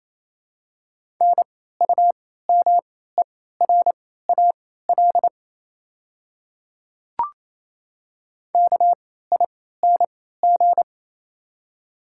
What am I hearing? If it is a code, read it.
Morse code, heard as a keyed tone, about 28 wpm, spaced as for 16 wpm, 700 Hz. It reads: NUMERAL KING
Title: 3x Character Spacing - Top 500 Words - 28wpm